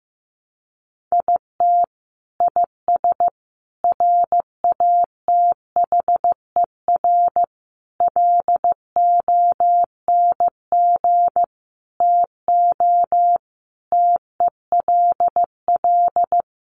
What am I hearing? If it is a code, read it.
Morse code, heard as a keyed tone, about 15 wpm, 700 Hz. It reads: IT IS RATHER LONG TO TELL
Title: Sentences from Top 700 Words 15wpm